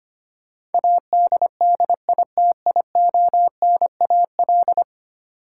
Morse code, heard as a keyed tone, about 25 wpm, 700 Hz. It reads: ADDITIONAL
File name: 10 Letter Words - First 3 Letters Given - 25wpm